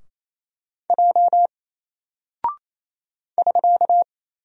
Morse code, heard as a keyed tone, about 28 wpm, 700 Hz. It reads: J <SK>
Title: Single Character (Letter:Number:Punctuation:Prosign) 28wpm